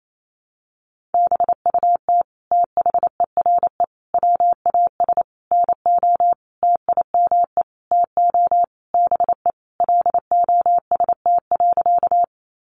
Morse code, heard as a keyed tone, about 28 wpm, 700 Hz. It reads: BUT THERE WAS NO TIME TO BE LOST.
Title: Sentences from Top 1400 Words 28wpm